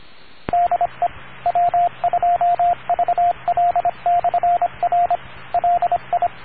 Morse code, heard as a keyed tone, about 26 wpm, 700 Hz, with radio noise. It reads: DE W2VL/R LI